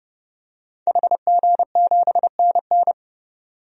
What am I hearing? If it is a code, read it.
Morse code, heard as a keyed tone, about 32 wpm, 700 Hz. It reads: HG7NN
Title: DXCC-100 Call Signs - 5 Characters - 30wpm